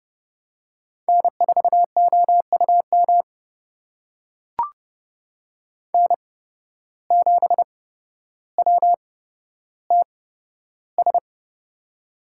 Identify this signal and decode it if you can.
Morse code, heard as a keyed tone, about 30 wpm, spaced as for 13 wpm, 700 Hz. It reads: N4OUM N7WTS
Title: USA Call Signs - 5 Characters - 8x Character Spacing - 30wpm